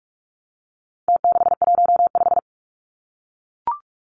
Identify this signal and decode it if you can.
Morse code, heard as a keyed tone, about 44 wpm, 700 Hz. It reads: T615
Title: Numbers - Contest Style - 4-Digit 45wpm